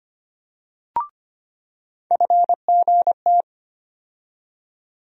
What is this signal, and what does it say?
Morse code, heard as a keyed tone, about 25 wpm, 700 Hz. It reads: FGT
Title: USA Call Sign Postfixes - 3 Letters 25wpm